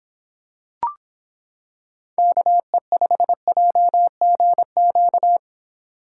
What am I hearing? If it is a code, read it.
Morse code, heard as a keyed tone, about 26 wpm, 700 Hz. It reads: KE5JGQ